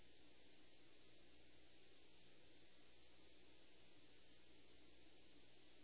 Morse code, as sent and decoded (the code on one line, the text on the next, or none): none